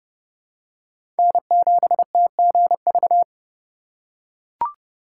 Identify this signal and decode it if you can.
Morse code, heard as a keyed tone, about 32 wpm, 690 Hz. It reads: N7TGV
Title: USA Call Signs - 3x Character Spacing - 30wpm